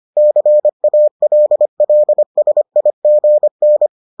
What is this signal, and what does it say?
Morse code, heard as a keyed tone, about 25 wpm, 600 Hz. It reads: CALLSIGN